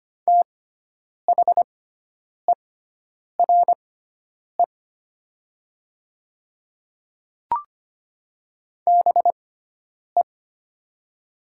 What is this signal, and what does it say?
Morse code, heard as a keyed tone, about 25 wpm, spaced as for 9 wpm, 690 Hz. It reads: THERE BE
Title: 6x Character Spacing - Top 100 Words - 25wpm